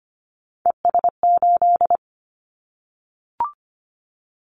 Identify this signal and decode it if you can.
Morse code, heard as a keyed tone, about 25 wpm, 700 Hz. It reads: ES8
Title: Callsign Prefixes for Top 100 DXCC - 25wpm